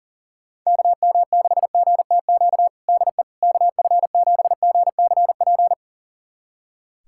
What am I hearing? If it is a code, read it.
Morse code, heard as a keyed tone, about 40 wpm, 700 Hz. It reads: KM6GTQ DE KF7GCP
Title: USA Callsign DE Callsign - 6 Characters - 40wpm